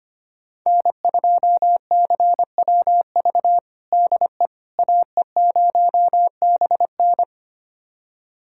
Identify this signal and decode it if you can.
Morse code, heard as a keyed tone, about 25 wpm, 700 Hz. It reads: N2CWV DE AE0BN
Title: USA Callsign DE Callsign - 5 Characters - 25wpm